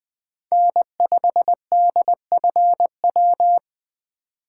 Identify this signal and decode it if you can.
Morse code, heard as a keyed tone, about 20 wpm, 700 Hz. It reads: N5DFW
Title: USA Call Signs - 5 Characters - 8x Character Spacing - 20wpm